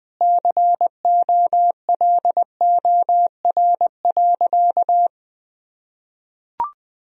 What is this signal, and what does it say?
Morse code, heard as a keyed tone, about 20 wpm, 700 Hz. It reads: COLOR.
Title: Sentences from Top 1000 Words - Review Part 2 - 20wpm